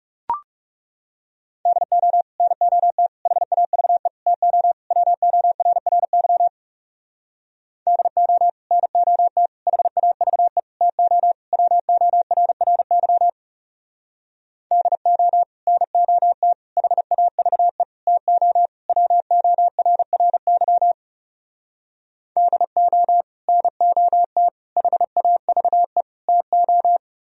Morse code, heard as a keyed tone, about 38 wpm, 700 Hz. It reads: DO NOT HAVE TO WORRY DO NOT HAVE TO WORRY DO NOT HAVE TO WORRY DO NOT HAVE TO
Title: Speed-Racing - Sets of 5 Words 30wpm